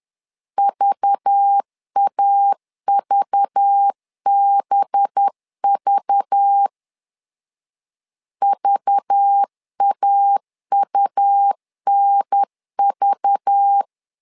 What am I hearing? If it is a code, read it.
Morse code, heard as a keyed tone, about 10 wpm, 800 Hz. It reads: VAVBV VAUNV
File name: LEZIONI / 5 LBVC / 5 LBV 10wpm 2248